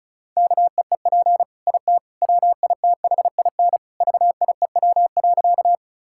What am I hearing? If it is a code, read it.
Morse code, heard as a keyed tone, about 34 wpm, 700 Hz. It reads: KEEP IT WITHIN VIEW.
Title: Sentences from Top 1000 Words - Review Part 6 - 35wpm